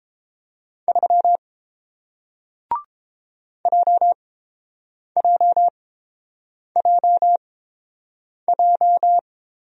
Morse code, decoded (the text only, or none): 3 JJJJ